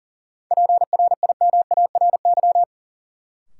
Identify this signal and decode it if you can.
Morse code, heard as a keyed tone, about 40 wpm, 700 Hz. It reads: PRIMARY
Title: Sentences from Top 1100 Words - New Word Drill Alt - 40wpm